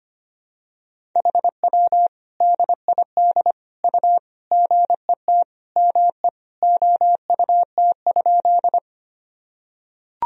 Morse code, heard as a keyed tone, about 25 wpm, 700 Hz. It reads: HW DID U GET ME OUT?
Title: Sentences from Top 200 Words and Abbreviations 25wpm